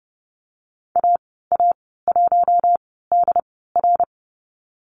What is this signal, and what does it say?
Morse code, heard as a keyed tone, about 32 wpm, spaced as for 17 wpm, 700 Hz. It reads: AA1DR
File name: Lesson 19g - Letter w - Call Signs 30wpm